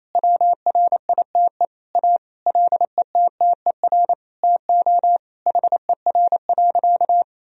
Morse code, heard as a keyed tone, about 28 wpm, 700 Hz. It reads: WRITE A LETTER TO HER.
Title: Sentences from Top 1000 Words - Review Part 3 - 28wpm